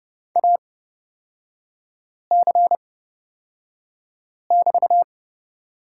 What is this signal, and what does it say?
Morse code, heard as a keyed tone, about 30 wpm, spaced as for 5 wpm, 700 Hz. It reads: AC<BT>
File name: Lesson 36e - BT - Three Characters 30wpm